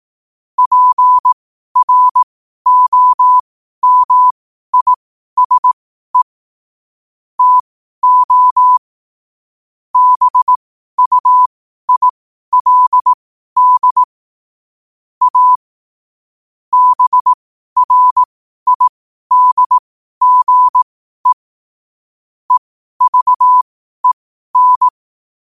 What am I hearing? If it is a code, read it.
Morse code, heard as a keyed tone, about 18 wpm, spaced as for 12 wpm, 1000 Hz. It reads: PROMISE TO BUILD A BRIDGE EVEN
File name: qotdfiles / 12wpm